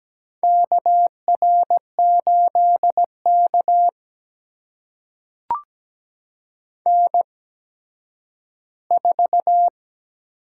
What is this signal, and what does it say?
Morse code, heard as a keyed tone, about 17 wpm, spaced as for 6 wpm, 700 Hz. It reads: KR8 KN4
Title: USA Call Signs - 4 Characters - 8x Character Spacing - 17wpm